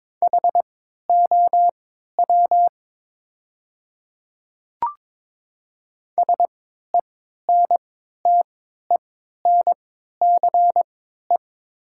Morse code, heard as a keyed tone, about 21 wpm, spaced as for 12 wpm, 700 Hz. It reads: HOW SENTENCE
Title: 3x Character Spacing - Top 200 Words - 22wpm